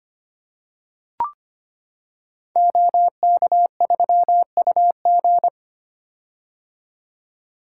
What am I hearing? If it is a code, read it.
Morse code, heard as a keyed tone, about 25 wpm, 700 Hz. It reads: OK3UG
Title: Code Groups - 5 Letter:Numbers - 25wpm